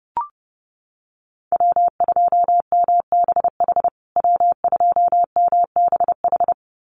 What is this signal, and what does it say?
Morse code, heard as a keyed tone, about 30 wpm, 700 Hz. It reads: W2MBH W2MBH
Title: USA Call Signs Repeated 2x 30wpm